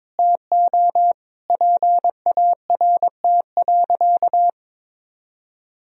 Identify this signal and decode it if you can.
Morse code, heard as a keyed tone, about 22 wpm, 700 Hz. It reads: TO PART.
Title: Sentences from Top 1000 Words - Review Part 1 - 22wpm